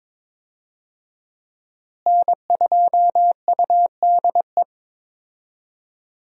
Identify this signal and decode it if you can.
Morse code, heard as a keyed tone, about 22 wpm, 700 Hz. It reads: N2UDE
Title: USA Call Signs - 5 Characters - Repeated 3x - 22wpm